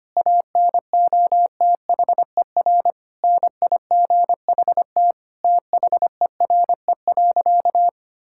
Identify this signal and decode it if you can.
Morse code, heard as a keyed tone, about 25 wpm, 700 Hz. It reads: ANOTHER NIGHT THERE.
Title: Sentences from Top 1000 Words - Review Part 5 - 25wpm